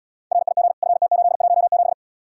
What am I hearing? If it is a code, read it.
Morse code, heard as a keyed tone, about 79 wpm, 700 Hz. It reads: RIG FT817